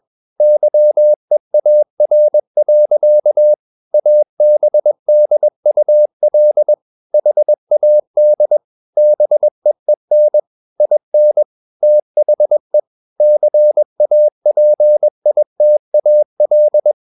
Morse code, heard as a keyed tone, about 21 wpm, 600 Hz. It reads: YEAR. ABDUL HAD BEEN IN THE CAPITAL
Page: CW med Gnister